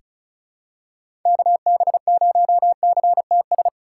Morse code, heard as a keyed tone, about 36 wpm, 700 Hz. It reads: KB0CTS
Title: USA Call Signs - 6 Characters - 2x Character Spacing - 35wpm